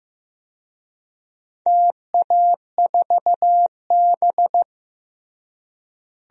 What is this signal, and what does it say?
Morse code, heard as a keyed tone, about 15 wpm, 700 Hz. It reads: TA4B
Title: DXCC-100 Call Signs - 4 Characters - Repeated 2x - 15wpm